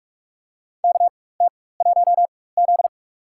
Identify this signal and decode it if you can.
Morse code, heard as a keyed tone, about 45 wpm, spaced as for 21 wpm, 700 Hz. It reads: KT1Z
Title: USA Call Signs - 4 Characters - 4x Character Spacing - 45wpm